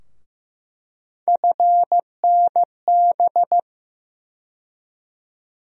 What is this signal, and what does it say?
Morse code, heard as a keyed tone, about 15 wpm, 700 Hz. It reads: FNB
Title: USA Call Sign Postfixes - 3 Letters 15wpm